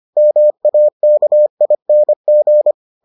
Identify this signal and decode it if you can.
Morse code, heard as a keyed tone, about 25 wpm, 600 Hz. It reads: MAKING